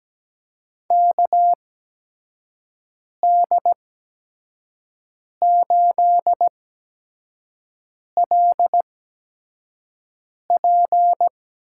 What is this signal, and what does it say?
Morse code, heard as a keyed tone, about 17 wpm, spaced as for 5 wpm, 700 Hz. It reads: KD8LP
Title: USA Call Signs - 6 Characters - 8x Character Spacing - 17wpm